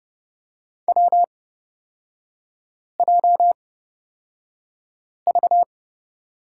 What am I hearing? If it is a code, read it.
Morse code, heard as a keyed tone, about 29 wpm, spaced as for 5 wpm, 690 Hz. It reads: WJV
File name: course / 30:10wpm / 236 Lesson 38e - Letter q - Three Characters 30wpm